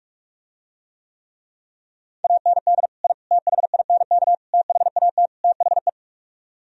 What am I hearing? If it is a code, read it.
Morse code, heard as a keyed tone, about 49 wpm, 700 Hz. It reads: AND I THINK THAT THE